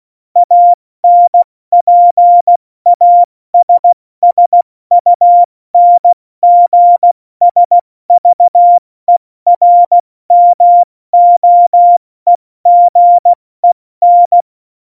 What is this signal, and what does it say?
Morse code, heard as a keyed tone, about 16 wpm, 700 Hz. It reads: ANPASSUNGSVERMOEGEN
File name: Morsekurs / Witze / Witz 14. Mai 2019, 70 F80